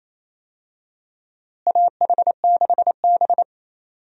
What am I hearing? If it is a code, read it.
Morse code, heard as a keyed tone, about 28 wpm, 700 Hz. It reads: AH6B
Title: USA Call Signs - 4 Characters - Repeated 2x - 28wpm